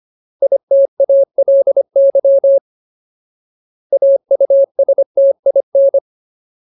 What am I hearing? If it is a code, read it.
Morse code, heard as a keyed tone, about 25 wpm, 550 Hz. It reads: ITALY AUSTIN
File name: Ultimate Word List / 25WPM Ultimate